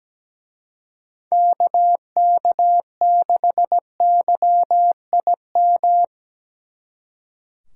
Morse code, heard as a keyed tone, about 19 wpm, 700 Hz. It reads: KK6YIM
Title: Speed Racing - USA 2x3 Call Signs 17wpm